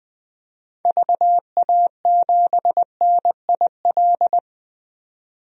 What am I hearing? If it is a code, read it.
Morse code, heard as a keyed tone, about 20 wpm, 700 Hz. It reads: VA7NIL
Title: DXCC-100 Call Signs - 6 Characters - 20wpm